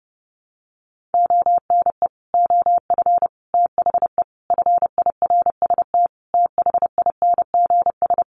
Morse code, heard as a keyed tone, about 30 wpm, 700 Hz. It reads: ONE OF THE FIRST THINGS
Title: Sets of 5 Words Encore 30wpm